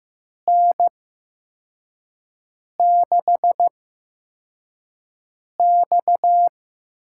Morse code, heard as a keyed tone, about 15 wpm, spaced as for 4 wpm, 700 Hz. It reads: N6X
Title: USA Call Signs - 5 Characters - 8x Character Spacing - 15wpm